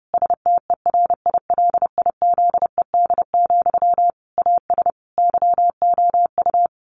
Morse code, heard as a keyed tone, about 30 wpm, 700 Hz. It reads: STERILIZED, AS YOU
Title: A Sound Of Thunder 30wpm